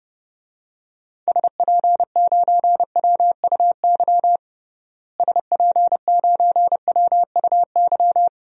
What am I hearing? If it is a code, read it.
Morse code, heard as a keyed tone, about 32 wpm, 700 Hz. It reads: SP9WUY SP9WUY
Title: DXCC-100 Call Signs - 6 Characters - Repeated 2x - 30wpm